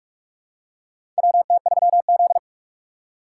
Morse code, heard as a keyed tone, about 42 wpm, 700 Hz. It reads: WT3Z
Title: USA Call Signs - 6x Character Spacing - 45wpm